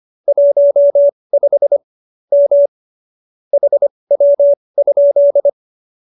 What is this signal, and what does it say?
Morse code, heard as a keyed tone, about 25 wpm, spaced as for 20 wpm, 570 Hz. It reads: 15 M HW?
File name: QSO310 20